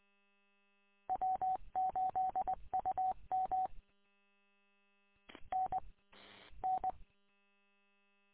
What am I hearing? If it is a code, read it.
Morse code, heard as a keyed tone, about 20 wpm, 750 Hz. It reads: W8UM N N